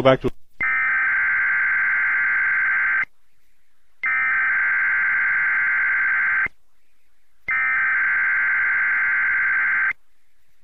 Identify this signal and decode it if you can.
Emergency Alert System (SAME) tones (header bursts), heard as harsh buzzing bursts, with speech. It is heard from 0.6 to 9.9 s.